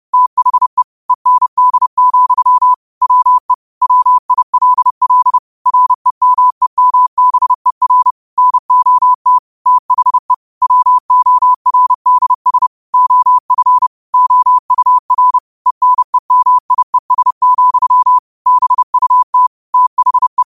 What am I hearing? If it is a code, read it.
Morse code, heard as a keyed tone, about 30 wpm, 1000 Hz. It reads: THE END, WE WILL REMEMBER NOT THE WORDS OF OUR ENEMIES, BUT THE
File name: qotdfiles / 30wpm